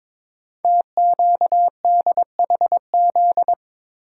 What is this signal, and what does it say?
Morse code, heard as a keyed tone, about 22 wpm, 700 Hz. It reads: TQDHZ
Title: Code Groups - 5 Letter:Numbers - 22wpm